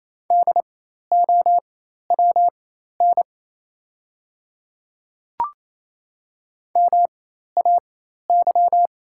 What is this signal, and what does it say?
Morse code, heard as a keyed tone, about 28 wpm, spaced as for 13 wpm, 700 Hz. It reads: DOWN MAY
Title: 4x Character Spacing - Top 200 Words - 28wpm